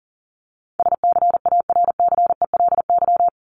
Morse code, heard as a keyed tone, about 37 wpm, 700 Hz. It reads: SCARCELY
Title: LY Suffix - Top 100 Words with Base Words 40wpm